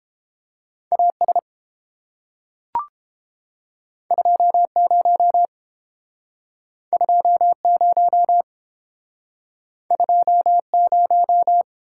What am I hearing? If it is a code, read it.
Morse code, heard as a keyed tone, about 31 wpm, 700 Hz. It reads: A S 2 0 2 0 2 0